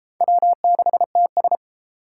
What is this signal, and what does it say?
Morse code, heard as a keyed tone, about 31 wpm, 700 Hz. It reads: W6TS